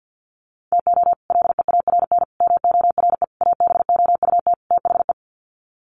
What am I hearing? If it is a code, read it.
Morse code, heard as a keyed tone, about 48 wpm, 700 Hz. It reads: TO LEARN MORE ABOUT THE
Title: Sets of 5 Words Encore2 50wpm